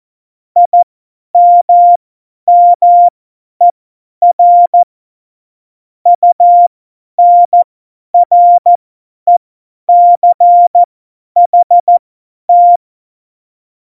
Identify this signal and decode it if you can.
Morse code, heard as a keyed tone, about 14 wpm, spaced as for 10 wpm, 700 Hz. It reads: IMMER UNRECHT
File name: Morsekurs / Witze / Witz 24.Dezember 2019, 50 F70